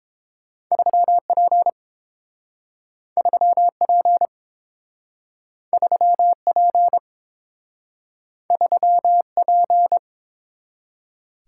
Morse code, heard as a keyed tone, about 30 wpm, 700 Hz. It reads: 3P 3P 3P 3P